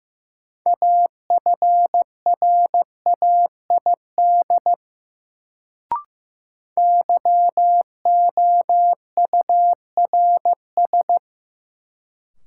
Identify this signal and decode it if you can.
Morse code, heard as a keyed tone, about 15 wpm, 700 Hz. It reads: AFRAID YOURS
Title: Sentences from Top 1500 Words - New Word Drill Alt - 15wpm